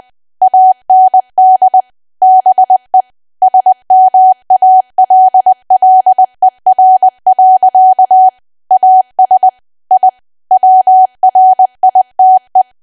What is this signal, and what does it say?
Morse code, heard as a keyed tone, about 20 wpm, 750 Hz. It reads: AND BE SMALLER. AS I WRITE
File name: cwa / W1AW-Level3 / 20WPM